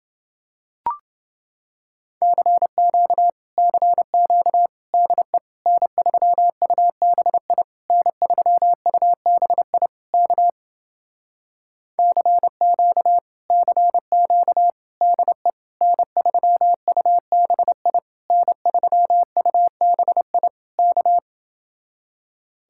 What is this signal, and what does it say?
Morse code, heard as a keyed tone, about 28 wpm, 700 Hz. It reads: CQ CQ DE N3UBI N3UBI K CQ CQ DE N3UBI N3UBI K